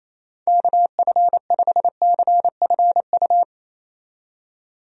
Morse code, heard as a keyed tone, about 28 wpm, 700 Hz. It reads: KF5CFU
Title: USA Call Signs - 6 Characters - 28wpm